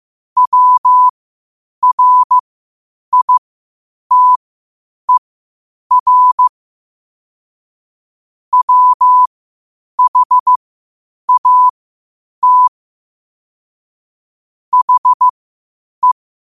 Morse code, heard as a keyed tone, about 15 wpm, spaced as for 8 wpm, 1000 Hz. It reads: WRITER WHAT HE
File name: qotdfiles / 8wpm